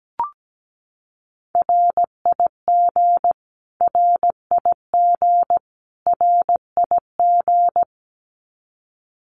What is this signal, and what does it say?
Morse code, heard as a keyed tone, about 17 wpm, 700 Hz. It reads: RIG RIG RIG
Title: Abbreviations Repeated 3x - 17wpm